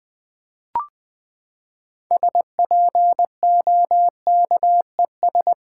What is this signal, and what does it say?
Morse code, heard as a keyed tone, about 20 wpm, 700 Hz. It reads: SPOKES
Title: Build A Word - 6 Letters At A Time - 12-Letter Words - 20wpm